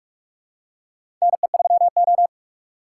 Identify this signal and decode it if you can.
Morse code, heard as a keyed tone, about 46 wpm, 700 Hz. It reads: NE3O
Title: USA Call Signs - 6x Character Spacing - 45wpm